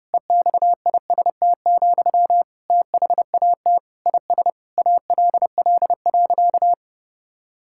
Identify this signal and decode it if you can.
Morse code, heard as a keyed tone, about 30 wpm, 700 Hz. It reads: EXIST, THAT IS ALL.
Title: Sentences from Top 1000 Words - Review Part 5 - 30wpm